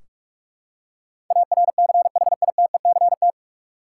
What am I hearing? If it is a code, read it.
Morse code, heard as a keyed tone, about 46 wpm, 700 Hz. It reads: ARCHITECT
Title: Practice QSO - Exchange 3 - More Info - 45wpm